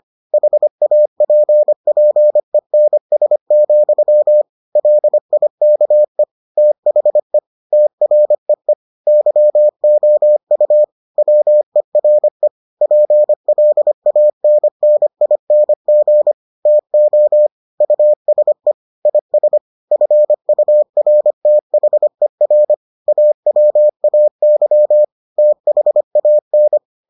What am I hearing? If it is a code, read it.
Morse code, heard as a keyed tone, about 25 wpm, 600 Hz. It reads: HAPPENS, LIKE THE TREE YOU WERE PLANNING TO USE IS FURTHER AWAY THAN